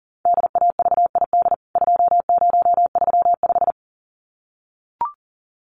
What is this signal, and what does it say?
Morse code, heard as a keyed tone, about 40 wpm, 700 Hz. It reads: DAVID 2035
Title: CWT Exchange 40wpm